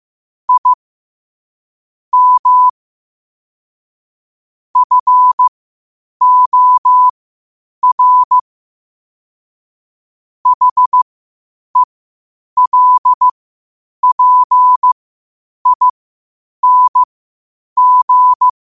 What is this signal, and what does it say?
Morse code, heard as a keyed tone, about 15 wpm, spaced as for 8 wpm, 1000 Hz. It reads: I M FOR HELPING
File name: qotdfiles / 8wpm